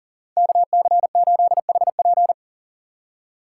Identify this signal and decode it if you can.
Morse code, heard as a keyed tone, about 40 wpm, 700 Hz. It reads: KC8HP
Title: USA Call Signs - 5 Characters - 2x Character Spacing - 40wpm